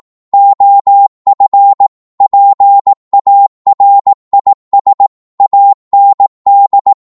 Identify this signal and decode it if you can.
Morse code, heard as a keyed tone, about 18 wpm, 800 Hz. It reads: OF PARIS AND